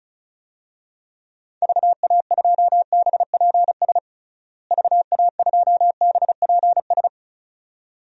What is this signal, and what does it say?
Morse code, heard as a keyed tone, about 36 wpm, 700 Hz. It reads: VA2BPS VA2BPS
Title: DXCC-100 Call Signs - 6 Characters - Repeated 2x - 35wpm